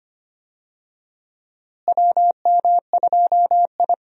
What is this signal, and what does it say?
Morse code, heard as a keyed tone, about 25 wpm, 700 Hz. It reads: WM2I
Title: USA Call Signs - 4 Characters - Repeated 2x - 25wpm